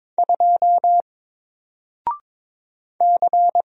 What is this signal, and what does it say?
Morse code, heard as a keyed tone, about 22 wpm, 700 Hz. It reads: 2 C
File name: Single-Letter-Number-22wpm